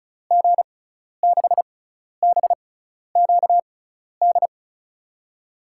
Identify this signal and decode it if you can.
Morse code, heard as a keyed tone, about 36 wpm, spaced as for 12 wpm, 700 Hz. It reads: G6BQD
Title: USA Call Signs - 6 Characters - 6x Character Spacing - 35wpm